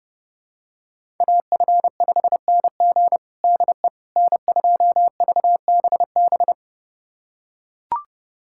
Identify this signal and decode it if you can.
Morse code, heard as a keyed tone, about 32 wpm, 700 Hz. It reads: AF5NG DE N2VBB
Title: USA Callsign DE Callsign - 5 Characters - 30wpm